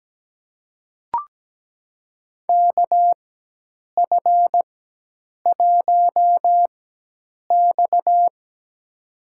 Morse code, heard as a keyed tone, about 17 wpm, spaced as for 8 wpm, 700 Hz. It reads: KF1X
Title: USA Call Signs - 4 Characters - 4x Character Spacing - 17wpm